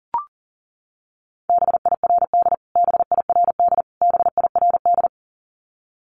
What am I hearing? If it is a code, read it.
Morse code, heard as a keyed tone, about 40 wpm, 700 Hz. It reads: BIRD BIRD BIRD
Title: Top 500 Words Repeated 40wpm